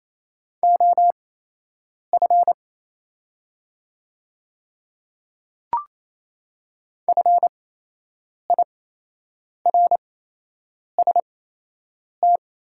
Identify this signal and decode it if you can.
Morse code, heard as a keyed tone, about 28 wpm, spaced as for 8 wpm, 700 Hz. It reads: OF FIRST
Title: 8x Character Spacing - Top 100 Words - 28wpm